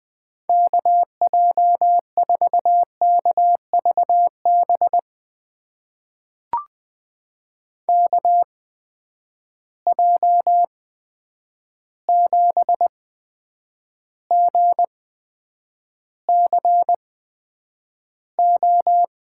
Morse code, heard as a keyed tone, about 20 wpm, spaced as for 9 wpm, 700 Hz. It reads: KJ4KVB KJ7GCO
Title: USA Call Signs - 6 Characters - 8x Character Spacing - 20wpm